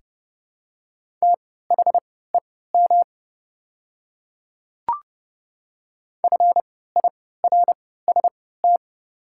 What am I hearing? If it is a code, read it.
Morse code, heard as a keyed tone, about 30 wpm, spaced as for 17 wpm, 700 Hz. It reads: THEM FIRST
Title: 3x Character Spacing - Top 100 Words - 30wpm